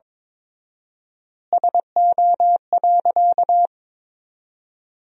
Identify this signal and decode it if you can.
Morse code, heard as a keyed tone, about 22 wpm, 700 Hz. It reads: SO.
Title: 4x Word-Spacing - Sentences from Top 200 Words - 22wpm